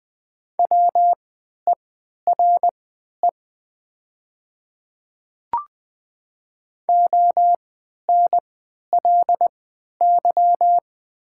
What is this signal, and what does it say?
Morse code, heard as a keyed tone, about 20 wpm, spaced as for 11 wpm, 700 Hz. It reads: WERE ONLY
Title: 3x Character Spacing - Top 500 Words - 20wpm